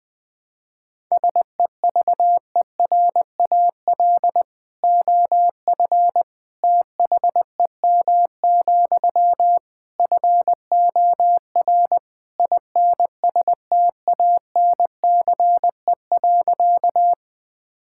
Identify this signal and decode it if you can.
Morse code, heard as a keyed tone, about 20 wpm, 700 Hz. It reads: SEVERAL OF THEM, FOR INSTANCE.
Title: Sentences from Top 1000 Words - Review Part 5 - 20wpm